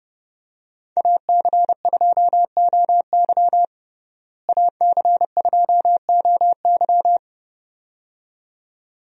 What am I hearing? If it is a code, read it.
Morse code, heard as a keyed tone, about 30 wpm, 700 Hz. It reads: AC2OY AC2OY
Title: USA Call Signs - 5 Characters - Repeated 2x - 30wpm